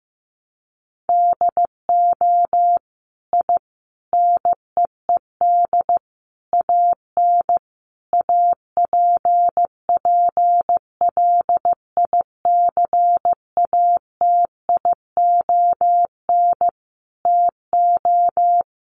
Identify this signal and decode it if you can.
Morse code, heard as a keyed tone, about 15 wpm, 700 Hz. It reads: DO I NEED AN APPLICATION TO
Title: Sentences from Top 1200 Words 15wpm